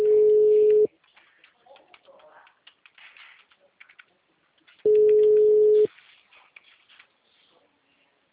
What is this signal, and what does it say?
Ringback tone (Europe, 425 Hz).